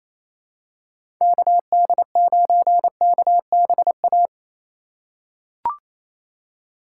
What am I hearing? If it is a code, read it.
Morse code, heard as a keyed tone, about 28 wpm, 690 Hz. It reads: KD9KBA
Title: USA Call Signs - 6 Characters - 2x Character Spacing - 28wpm